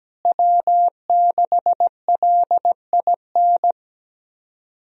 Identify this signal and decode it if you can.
Morse code, heard as a keyed tone, about 17 wpm, 700 Hz. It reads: W6LIN
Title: USA Call Signs - 5 Characters - Repeated 2x - 17wpm